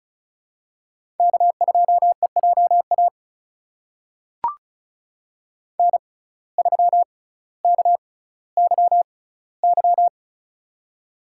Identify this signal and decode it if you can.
Morse code, heard as a keyed tone, about 36 wpm, spaced as for 18 wpm, 700 Hz. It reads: K2EJA N3KYY
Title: USA Call Signs - 5 Characters - 6x Character Spacing - 35wpm